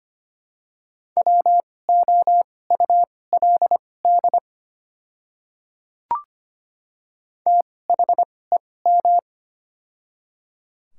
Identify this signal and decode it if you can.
Morse code, heard as a keyed tone, about 25 wpm, spaced as for 18 wpm, 700 Hz. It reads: WOULD THEM
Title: 2x Character Spacing - Top 100 Words - 25wpm